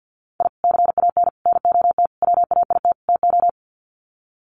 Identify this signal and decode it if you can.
Morse code, heard as a keyed tone, about 50 wpm, 700 Hz. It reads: I CAN NOT WAIT TO
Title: Sets of 5 Words Encore 50wpm